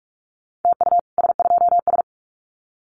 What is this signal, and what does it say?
Morse code, heard as a keyed tone, about 44 wpm, 700 Hz. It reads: TU S2S
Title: Common QSO Phrases Encore 45wpm